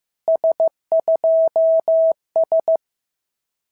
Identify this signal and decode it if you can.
Morse code, heard as a keyed tone, about 14 wpm, 650 Hz. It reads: S2S
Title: SOTA QSO Practice 15wpm